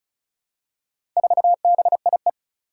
Morse code, heard as a keyed tone, about 36 wpm, 700 Hz. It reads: 4BIE
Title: Four Challenging Letter-Number Combinations 35wpm